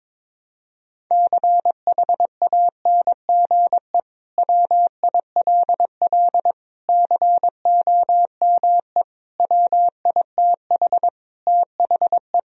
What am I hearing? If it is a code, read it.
Morse code, heard as a keyed tone, about 22 wpm, 700 Hz. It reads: CHANGE WILL COME WITH THE
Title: Sentences from Top 1000 Words - Review Part 2 - 22wpm